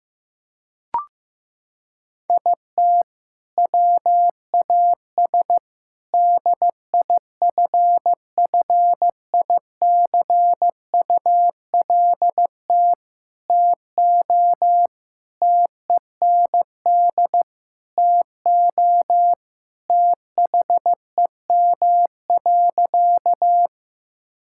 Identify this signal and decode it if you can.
Morse code, heard as a keyed tone, about 15 wpm, 700 Hz. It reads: IT WAS DIFFICULT TO TEND TO THEM.
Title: Sentences from Top 1000 Words - Review Part 5 - 15wpm